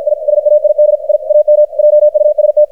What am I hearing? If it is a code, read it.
Morse code, heard as a keyed tone, about 52 wpm, 600 Hz, with radio noise. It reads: I SAID I AM JUST
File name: archived / CRSnet